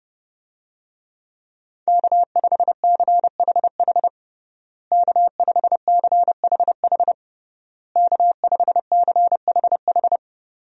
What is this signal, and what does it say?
Morse code, heard as a keyed tone, about 30 wpm, 700 Hz. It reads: K5CHH K5CHH K5CHH
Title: USA Call Signs - 5 Characters - Repeated 3x - 30wpm